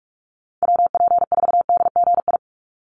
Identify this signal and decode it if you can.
Morse code, heard as a keyed tone, about 44 wpm, 700 Hz. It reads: WP4DGI
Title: USA Call Signs 45wpm